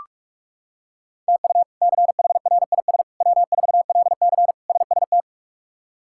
Morse code, heard as a keyed tone, about 42 wpm, 700 Hz. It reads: TU CHRIS W4LC SST
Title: K1USN SST QSO Practice - 45wpm